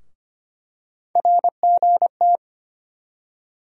Morse code, heard as a keyed tone, about 25 wpm, 700 Hz. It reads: RGT
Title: USA Call Sign Postfixes - 3 Letters 25wpm